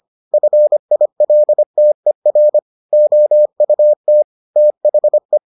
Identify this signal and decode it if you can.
Morse code, heard as a keyed tone, about 25 wpm, 600 Hz. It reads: FILTER OUT THE